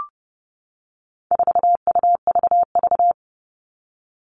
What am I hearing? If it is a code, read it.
Morse code, heard as a keyed tone, about 30 wpm, 700 Hz. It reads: 4UVV
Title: All Cut Numbers - 4-Digit 30wpm